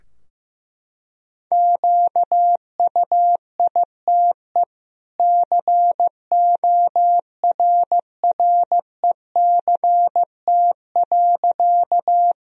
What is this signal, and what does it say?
Morse code, heard as a keyed tone, about 14 wpm, 700 Hz. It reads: QUITE CORRECT.
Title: Sentences from Top 1000 Words - Review Part 2 - 15wpm